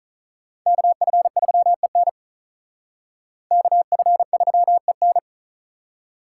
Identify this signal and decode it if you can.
Morse code, heard as a keyed tone, about 45 wpm, 700 Hz. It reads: KF3EN KF3EN